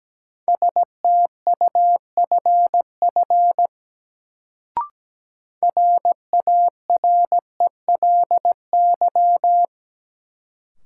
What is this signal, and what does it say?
Morse code, heard as a keyed tone, about 17 wpm, 700 Hz. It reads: STUFF RARELY
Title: Sentences from Top 1400 Words - New Word Drill - 17wpm